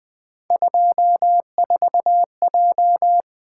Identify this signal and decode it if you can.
Morse code, heard as a keyed tone, about 23 wpm, 700 Hz. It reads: 24J
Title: Speed-Racing - Three Letter-Number Combinations - 20wpm